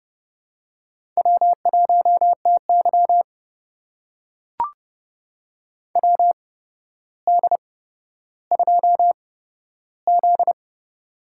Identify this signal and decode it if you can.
Morse code, heard as a keyed tone, about 30 wpm, spaced as for 13 wpm, 700 Hz. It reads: W1TY WD2Z
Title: USA Call Signs - 4 Characters - 8x Character Spacing - 30wpm